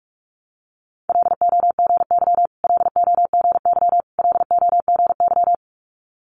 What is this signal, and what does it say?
Morse code, heard as a keyed tone, about 46 wpm, 690 Hz. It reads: LOGY LOGY LOGY
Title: Multi-letter Phonemes Repeated 3x 45wpm